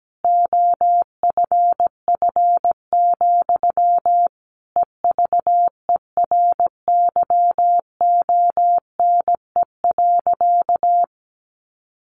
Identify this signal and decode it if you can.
Morse code, heard as a keyed tone, about 17 wpm, 700 Hz. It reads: OFF, EVERYONE.
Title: A Sound Of Thunder 17wpm